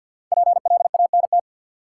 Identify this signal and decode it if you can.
Morse code, heard as a keyed tone, about 52 wpm, 700 Hz. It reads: PLANT
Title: Sentences from Top 1000 Words - Review Part 1 - New Word Drill - 50wpm